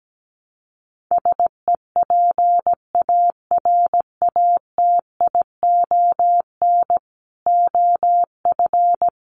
Morse code, heard as a keyed tone, about 17 wpm, 700 Hz. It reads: SEPARATION OF
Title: Sets of 5 Words 17wpm